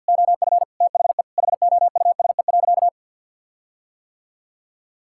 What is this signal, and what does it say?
Morse code, heard as a keyed tone, about 50 wpm, 700 Hz. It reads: OF THE HOUSE.
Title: Sentences from Top 1600 Words 50wpm